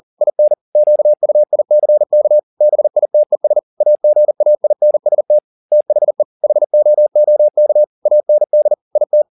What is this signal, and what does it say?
Morse code, heard as a keyed tone, about 40 wpm, 600 Hz. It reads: IN QUICK BITES AGAINST THE HOOK AND IT
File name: the old man and the sea 2 40wpm